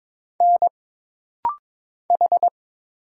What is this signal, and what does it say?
Morse code, heard as a keyed tone, about 22 wpm, 700 Hz. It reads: N H
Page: This rapid-fire timing is geared to push your ICR (Instant Character Recognition) skills.